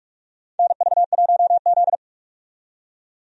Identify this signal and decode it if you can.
Morse code, heard as a keyed tone, about 46 wpm, 700 Hz. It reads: NV1Z
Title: USA Call Signs - 4 Characters - 45wpm